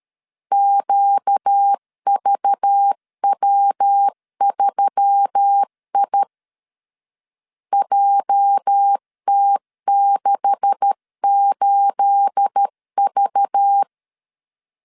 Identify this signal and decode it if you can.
Morse code, heard as a keyed tone, about 13 wpm, 800 Hz. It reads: QVW3I JT68V
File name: LEZIONI / 9 286 12wpm 2305